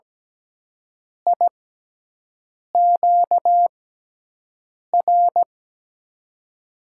Morse code, heard as a keyed tone, about 17 wpm, spaced as for 6 wpm, 700 Hz. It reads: IQR
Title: USA Call Signs - 6x Character Spacing - 17wpm